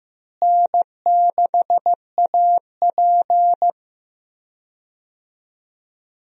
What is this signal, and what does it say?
Morse code, heard as a keyed tone, about 15 wpm, 700 Hz. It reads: N6AP
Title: USA Call Signs - 4 Characters - Repeated 2x - 15wpm